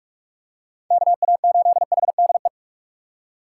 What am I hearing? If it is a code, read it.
Morse code, heard as a keyed tone, about 46 wpm, 700 Hz. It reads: KA8HDE
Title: K1USN SST QSO Practice - 45wpm